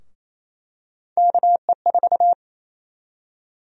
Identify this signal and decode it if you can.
Morse code, heard as a keyed tone, about 28 wpm, 700 Hz. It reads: KE4
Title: USA Call Sign Prefixes - 28wpm